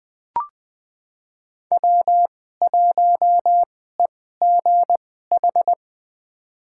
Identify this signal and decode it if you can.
Morse code, heard as a keyed tone, about 20 wpm, spaced as for 14 wpm, 700 Hz. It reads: W1EGH
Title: USA Call Signs - 2x Character Spacing - 20wpm